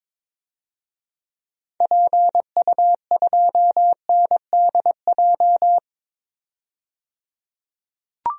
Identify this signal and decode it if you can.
Morse code, heard as a keyed tone, about 22 wpm, 700 Hz. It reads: PU2NDJ
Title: DXCC-100 Call Signs - 6 Characters - 22wpm